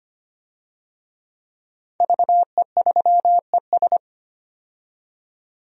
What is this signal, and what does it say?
Morse code, heard as a keyed tone, about 25 wpm, 700 Hz. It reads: VE3ES
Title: DXCC-100 Call Signs - 5 Characters - 25wpm